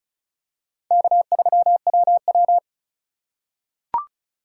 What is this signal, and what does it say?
Morse code, heard as a keyed tone, about 34 wpm, 700 Hz. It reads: K3WW
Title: USA Call Signs - 2x Character Spacing - 35wpm